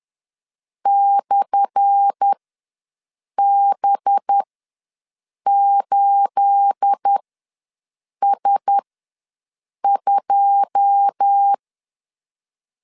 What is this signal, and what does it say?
Morse code, heard as a keyed tone, about 10 wpm, spaced as for 6 wpm, 800 Hz. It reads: /B8S2
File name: LEZIONI / 10 SIGNS / 10A SLASH ws 10wpm 4121